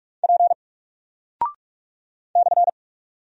Morse code, heard as a keyed tone, about 44 wpm, 700 Hz. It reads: P /
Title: Single Character (Letter:Number:Punctuation:Prosign) Alt 45wpm